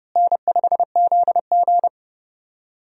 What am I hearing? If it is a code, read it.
Morse code, heard as a keyed tone, about 31 wpm, 700 Hz. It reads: N5ZG
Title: USA Call Signs - 4 Characters - Repeated 2x - 30wpm